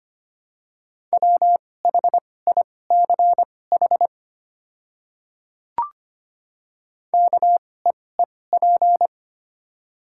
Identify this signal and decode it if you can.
Morse code, heard as a keyed tone, about 25 wpm, spaced as for 18 wpm, 700 Hz. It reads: WHICH KEEP
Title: 2x Character Spacing - Top 500 Words - 25wpm